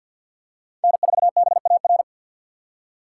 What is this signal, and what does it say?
Morse code, heard as a keyed tone, about 52 wpm, 690 Hz. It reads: N4BAR
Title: USA Call Signs - 5 Characters - 2x Character Spacing - 50wpm